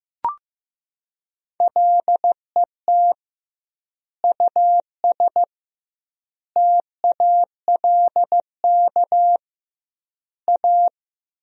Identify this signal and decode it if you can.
Morse code, heard as a keyed tone, about 16 wpm, 700 Hz. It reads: LET US TALK A
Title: 2x Word-Spacing - Sentences from Top 400 Words - 15wpm